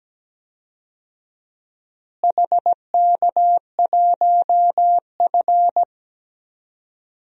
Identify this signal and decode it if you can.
Morse code, heard as a keyed tone, about 17 wpm, 700 Hz. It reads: HK1F
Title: DXCC-100 Call Signs - 4 Characters - Repeated 3x - 17wpm